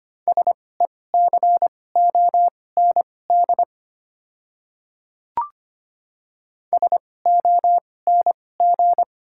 Morse code, heard as a keyed tone, about 25 wpm, spaced as for 18 wpm, 690 Hz. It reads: SECOND SONG
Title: 2x Character Spacing - Top 500 Words - 25wpm